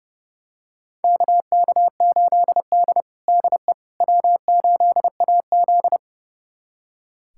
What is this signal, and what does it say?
Morse code, heard as a keyed tone, about 30 wpm, 700 Hz. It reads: KK8D DE W8AZ
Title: USA Callsign DE Callsign - 4 Characters - 30wpm